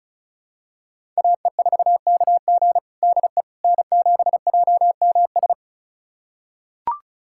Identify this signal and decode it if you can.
Morse code, heard as a keyed tone, about 36 wpm, 700 Hz. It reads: AE4KG DE N7JMS
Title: USA Callsign DE Callsign - 5 Characters - 35wpm